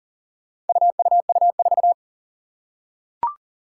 Morse code, heard as a keyed tone, about 40 wpm, 700 Hz. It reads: UUU4
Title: Four Challenging Letter-Number Combinations 40wpm